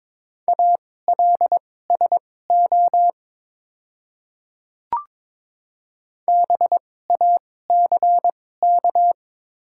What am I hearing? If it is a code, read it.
Morse code, heard as a keyed tone, about 22 wpm, spaced as for 16 wpm, 700 Hz. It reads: ALSO BACK
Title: 2x Character Spacing - Top 200 Words - 22wpm